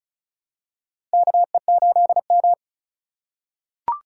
Morse code, heard as a keyed tone, about 34 wpm, 700 Hz. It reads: KE8M
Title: K1USN SST QSO Practice - 35wpm